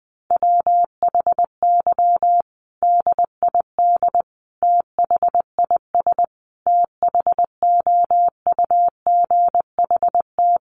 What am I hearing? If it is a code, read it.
Morse code, heard as a keyed tone, about 19 wpm, 700 Hz. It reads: WHY DID THIS THOUGHT
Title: Sentences from Top 900 Words 20wpm